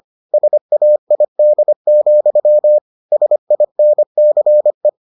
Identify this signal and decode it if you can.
Morse code, heard as a keyed tone, about 25 wpm, 600 Hz. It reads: SAID, SINCE